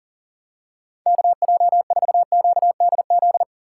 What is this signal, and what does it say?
Morse code, heard as a keyed tone, about 40 wpm, 700 Hz. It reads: KJ4QDZ
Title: USA Call Signs - 4x Character Spacing - 40wpm